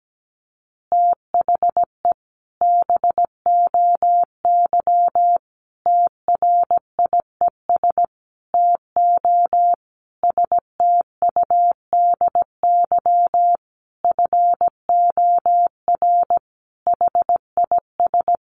Morse code, heard as a keyed tone, about 17 wpm, 700 Hz. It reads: THE BOY TRIES TO STUDY FOR HIS
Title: Additional Easy Phrases 17wpm